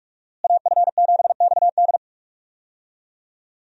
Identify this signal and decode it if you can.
Morse code, heard as a keyed tone, about 42 wpm, 700 Hz. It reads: AF7XD
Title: USA Call Signs - 5 Characters - Repeated 2x - 45wpm